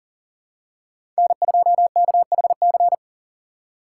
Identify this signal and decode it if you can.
Morse code, heard as a keyed tone, about 39 wpm, 700 Hz. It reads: N2KHC
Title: USA Call Signs - 5 Characters - 3x Character Spacing - 40wpm